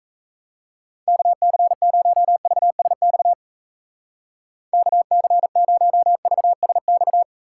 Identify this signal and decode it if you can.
Morse code, heard as a keyed tone, about 36 wpm, 700 Hz. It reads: KC0VSX KC0VSX